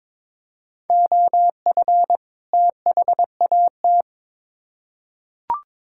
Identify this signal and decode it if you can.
Morse code, heard as a keyed tone, about 22 wpm, 700 Hz. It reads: OF THAT
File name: Build A Phrase - 2-Word Phrases - 22wpm